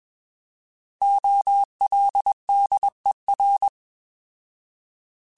Morse code, heard as a keyed tone, about 21 wpm, 800 Hz. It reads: OLDER
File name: Alphabetical / Alphabetical mp3s / Os @20